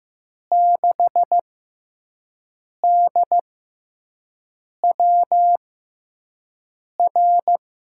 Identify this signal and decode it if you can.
Morse code, heard as a keyed tone, about 15 wpm, spaced as for 5 wpm, 700 Hz. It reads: 6DWR
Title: USA Call Signs - 5 Characters - 6x Character Spacing - 15wpm